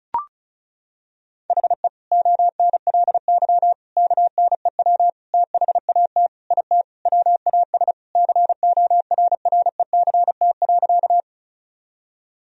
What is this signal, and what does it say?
Morse code, heard as a keyed tone, about 36 wpm, 700 Hz. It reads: HE ONLY KNEW THAT IT WAS CORRECT.
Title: Sentences from Top 1000 Words - Review Part 2 - 35wpm